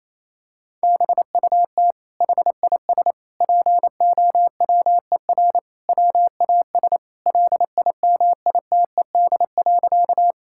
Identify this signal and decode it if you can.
Morse code, heard as a keyed tone, about 28 wpm, 700 Hz. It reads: BUT HIS POWER WAS LIMITED.
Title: Sentences from Top 1000 Words - Review Part 6 - 28wpm